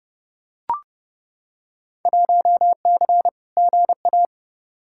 Morse code, heard as a keyed tone, about 29 wpm, 700 Hz. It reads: 1C GA
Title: Field Day Exchanges Encore - 30wpm